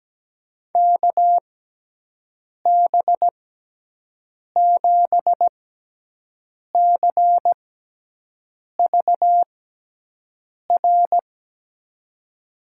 Morse code, heard as a keyed tone, about 17 wpm, spaced as for 6 wpm, 700 Hz. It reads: KB7CVR
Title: USA Call Signs - 6 Characters - 6x Character Spacing - 17wpm